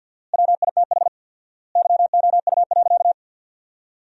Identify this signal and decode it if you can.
Morse code, heard as a keyed tone, about 53 wpm, 700 Hz. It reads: WITH YOU.
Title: 4x Word-Spacing - Sentences from Top 300 Words - 50wpm